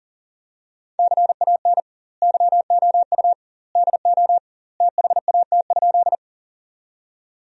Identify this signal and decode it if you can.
Morse code, heard as a keyed tone, about 37 wpm, 700 Hz. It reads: CAN YOU DO THAT?
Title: 2x Word-Spacing - Sentences from Top 100 Words - 40wpm